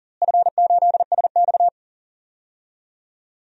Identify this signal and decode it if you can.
Morse code, heard as a keyed tone, about 40 wpm, 700 Hz. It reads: F8SX
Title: DXCC-100 Call Signs - 4 Characters - Repeated 2x - 40wpm